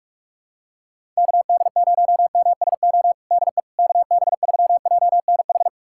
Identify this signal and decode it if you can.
Morse code, heard as a keyed tone, about 46 wpm, 700 Hz. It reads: KD0MSO DE KB3JNH
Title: USA Callsign DE Callsign - 6 Characters - 45wpm